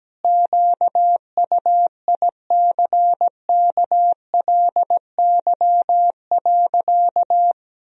Morse code, heard as a keyed tone, about 17 wpm, 700 Hz. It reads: QUICKLY.
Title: Sentences from Top 1000 Words - Review Part 1 - 17wpm